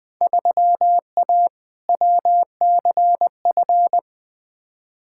Morse code, heard as a keyed tone, about 20 wpm, 700 Hz. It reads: 3A WCF
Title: Field Day Exchanges Encore - 20wpm